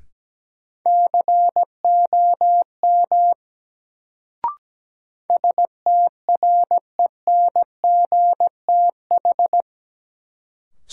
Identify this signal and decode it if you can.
Morse code, heard as a keyed tone, about 17 wpm, 700 Hz. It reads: COM STRENGTH
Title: Sentences from Top 1300 Words - New Word Drill Alt - 17wpm